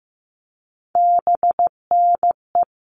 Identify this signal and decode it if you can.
Morse code, heard as a keyed tone, about 15 wpm, 700 Hz. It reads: BNE
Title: All Cut Numbers - 3-Digit 15wpm